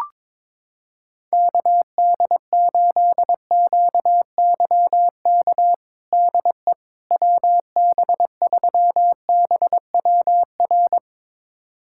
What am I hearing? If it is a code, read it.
Morse code, heard as a keyed tone, about 22 wpm, 700 Hz. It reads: KD8QYK DE WB3BWR
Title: USA Callsign DE Callsign 22wpm